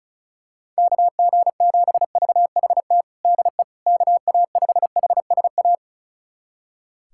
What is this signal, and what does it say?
Morse code, heard as a keyed tone, about 35 wpm, 700 Hz. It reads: KG7VHT DE KA5HSA
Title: USA Callsign DE Callsign - 6 Characters - 35wpm